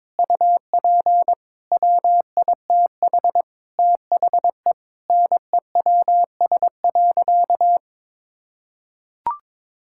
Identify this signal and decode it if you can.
Morse code, heard as a keyed tone, about 22 wpm, 700 Hz. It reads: UP WITH THE NEWS.
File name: practice / 22wpm / Sentences with 2-Word Phrases - 22wpm